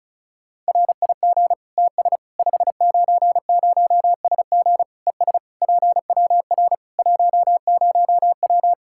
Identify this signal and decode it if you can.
Morse code, heard as a keyed tone, about 36 wpm, 700 Hz. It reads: RIG TS 590SG ES PWR 10W
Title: Practice QSO - Exchange 2 - Info - 35wpm